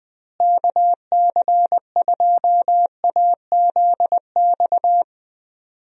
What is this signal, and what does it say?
Morse code, heard as a keyed tone, about 20 wpm, 700 Hz. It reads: KC2AZX
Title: USA Call Signs - 6 Characters - Repeated 3x - 20wpm